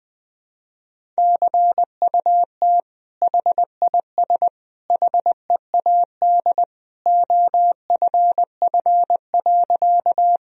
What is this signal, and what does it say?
Morse code, heard as a keyed tone, about 20 wpm, 700 Hz. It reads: CUT HIS HEAD OFF.
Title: Sentences from Top 1000 Words - Review Part 1 - 20wpm